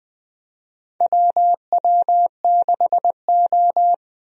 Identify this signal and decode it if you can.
Morse code, heard as a keyed tone, about 20 wpm, 700 Hz. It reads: WW6O
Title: USA Call Signs - 4 Characters - 20wpm